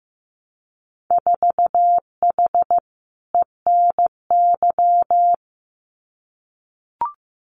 Morse code, heard as a keyed tone, about 15 wpm, 700 Hz. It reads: 4H ENY
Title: Winter Field Day Exchanges Encore 15wpm